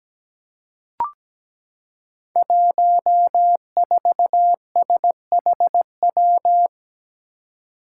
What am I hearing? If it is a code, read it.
Morse code, heard as a keyed tone, about 17 wpm, 700 Hz. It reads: 14SHW
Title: Five Challenging Letter-Number Combinations 17wpm